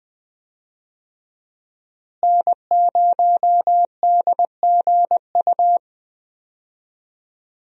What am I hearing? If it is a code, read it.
Morse code, heard as a keyed tone, about 20 wpm, 700 Hz. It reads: N0DGU
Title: USA Call Signs - 5 Characters - Repeated 3x - 20wpm